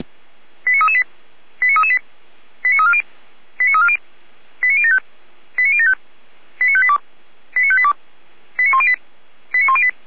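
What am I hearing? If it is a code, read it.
Five-tone selective call (ZVEI), heard as quick runs of five tones, tones of about 70 ms. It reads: 89338 89338 89340 89340 89075 89075 89582 89582 89108 89108
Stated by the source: ZVEI five-tone call received by SDR, from the feuerwehr-orlamuende.org server.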